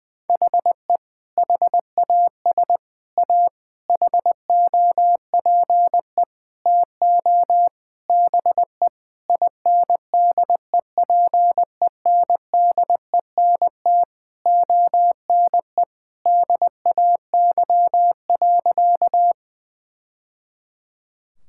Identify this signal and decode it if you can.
Morse code, heard as a keyed tone, about 21 wpm, 700 Hz. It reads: HE HAS A HOPE TO BE INDEPENDENT ONE DAY.
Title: Sentences from Top 1000 Words - Review Part 6 - 20wpm